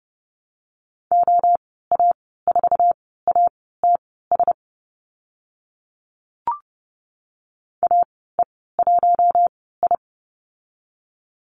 Morse code, heard as a keyed tone, about 30 wpm, spaced as for 17 wpm, 700 Hz. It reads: OA4ATS AE1I
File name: Lesson 14g - Number 2 - Call Signs 30wpm